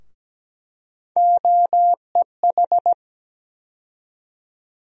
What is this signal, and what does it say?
Morse code, heard as a keyed tone, about 17 wpm, 700 Hz. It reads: OEH
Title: USA Call Sign Postfixes - 3 Letters 17wpm